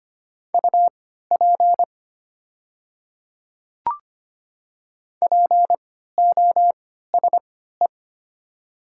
Morse code, heard as a keyed tone, about 25 wpm, spaced as for 14 wpm, 690 Hz. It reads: UP POSE
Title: 3x Character Spacing - Top 500 Words - 25wpm